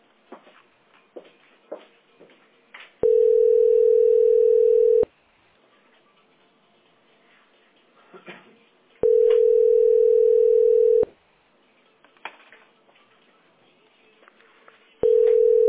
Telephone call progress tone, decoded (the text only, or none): ringback tone